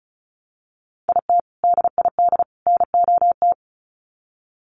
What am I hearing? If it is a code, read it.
Morse code, heard as a keyed tone, about 34 wpm, 700 Hz. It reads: IT DID NOT
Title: Sets of 3 Words 35wpm